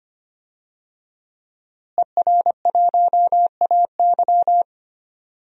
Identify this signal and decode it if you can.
Morse code, heard as a keyed tone, about 25 wpm, 700 Hz. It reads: ER1AY
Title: DXCC-100 Call Signs - 5 Characters - Repeated 2x - 25wpm